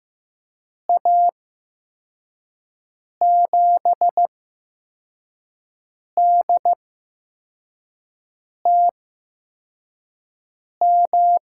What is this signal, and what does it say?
Morse code, heard as a keyed tone, about 15 wpm, spaced as for 4 wpm, 700 Hz. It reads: A7DTM
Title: USA Call Signs - 6 Characters - 8x Character Spacing - 15wpm